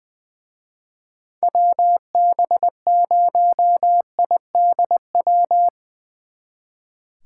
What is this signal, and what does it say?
Morse code, heard as a keyed tone, about 22 wpm, 700 Hz. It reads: WB0IDW
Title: Speed Racing - USA 2x3 Call Signs 20wpm